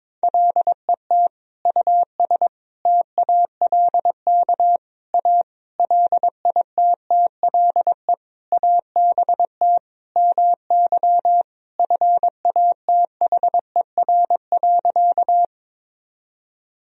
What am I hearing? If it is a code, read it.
Morse code, heard as a keyed tone, about 22 wpm, 700 Hz. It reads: LET US TALK A LITTLE ABT MY FATHER.
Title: Sentences from Top 400 Words and Abbreviations 22wpm